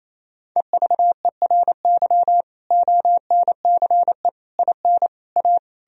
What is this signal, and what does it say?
Morse code, heard as a keyed tone, about 38 wpm, 700 Hz. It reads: EVERY ONCE IN A